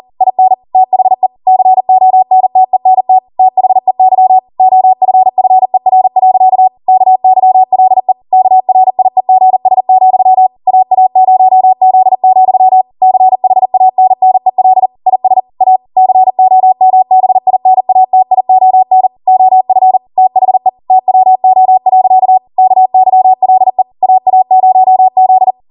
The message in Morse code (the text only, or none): IN THE CONTENT THEY OFFER. KYLE KRIEGS, AA0Z, CHANNEL IS A COMBINATION OF THE TWO. KYLE AA0Z